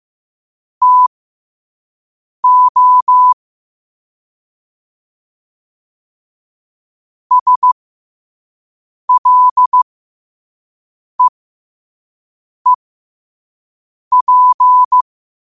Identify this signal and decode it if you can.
Morse code, heard as a keyed tone, about 15 wpm, spaced as for 5 wpm, 1000 Hz. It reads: TO SLEEP
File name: qotdfiles / 5wpm